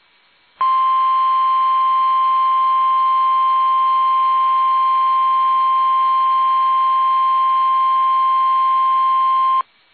Emergency Alert System (SAME) tones (the 1050 Hz attention tone). It is heard from 0.5 to 9.7 s.
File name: eas / KUSO / KUSO